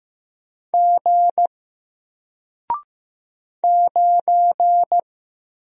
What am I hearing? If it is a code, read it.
Morse code, heard as a keyed tone, about 15 wpm, 700 Hz. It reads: G 9
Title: Single Letter-Number Alt 15wpm